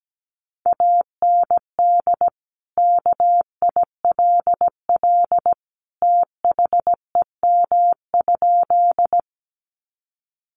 Morse code, heard as a keyed tone, about 17 wpm, 700 Hz. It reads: AND KILL THEM?
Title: Sentences from Top 1400 Words 17wpm